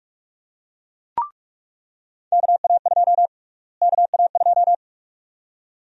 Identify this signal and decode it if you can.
Morse code, heard as a keyed tone, about 48 wpm, 700 Hz. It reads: KA2 KA2
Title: USA Call Sign Prefixes - Repeated 2x - 45wpm